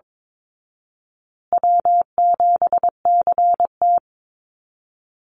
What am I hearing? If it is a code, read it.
Morse code, heard as a keyed tone, about 22 wpm, 700 Hz. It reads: W7CT
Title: Easy Call Signs Repeated 3x - 22wpm